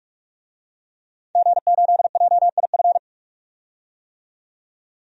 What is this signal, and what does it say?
Morse code, heard as a keyed tone, about 45 wpm, 700 Hz. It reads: G8JIF
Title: DXCC-100 Call Signs - 5 Characters - 45wpm